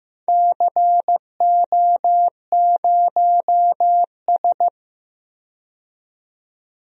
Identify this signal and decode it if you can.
Morse code, heard as a keyed tone, about 15 wpm, 700 Hz. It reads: CO0S
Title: DXCC-100 Call Signs - 4 Characters - 15wpm